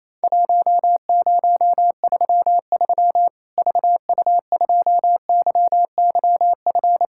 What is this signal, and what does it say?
Morse code, heard as a keyed tone, about 28 wpm, 700 Hz. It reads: 1033 VU2YYF
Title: Worked All Europe DX Contest - QTC Exchange - 28wpm